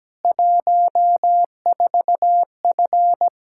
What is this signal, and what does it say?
Morse code, heard as a keyed tone, about 17 wpm, 700 Hz. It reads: 14F
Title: K1USN SST QSO Practice 17wpm